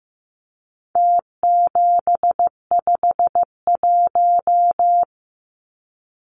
Morse code, heard as a keyed tone, about 15 wpm, 700 Hz. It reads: T751
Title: Numbers - Contest Style - 4-Digit 15wpm